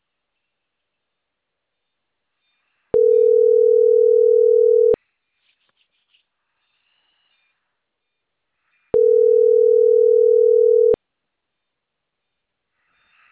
Ringback tone (North America), 440 + 480 Hz.
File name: CENTRAL CAEA / out agcaea06